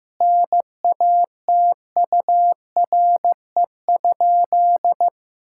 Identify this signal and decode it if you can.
Morse code, heard as a keyed tone, about 15 wpm, 700 Hz. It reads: NATURE?
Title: Sentences from Top 1000 Words - Review Part 5 - 15wpm